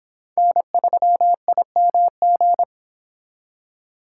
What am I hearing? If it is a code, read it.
Morse code, heard as a keyed tone, about 26 wpm, 700 Hz. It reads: N3IMG